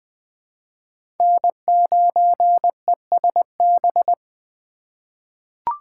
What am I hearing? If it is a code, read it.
Morse code, heard as a keyed tone, about 20 wpm, 700 Hz. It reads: N9ESB
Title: USA Call Signs - 5 Characters - 2x Character Spacing - 20wpm